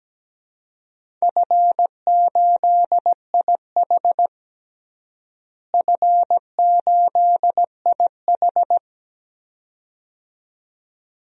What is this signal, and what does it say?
Morse code, heard as a keyed tone, about 17 wpm, 700 Hz. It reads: F8IH F8IH
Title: DXCC-100 Call Signs - 4 Characters - Repeated 2x - 17wpm